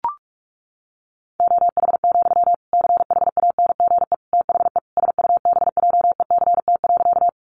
Morse code, heard as a keyed tone, about 44 wpm, 700 Hz. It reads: OH, CHANGE THE SUBJECT.
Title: Sentences from Top 800 Words 45wpm